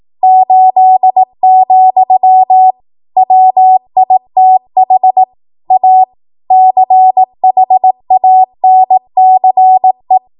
Morse code, heard as a keyed tone, about 18 wpm, 750 Hz. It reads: 8, WITH A CHANCE
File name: Morse / 18WPM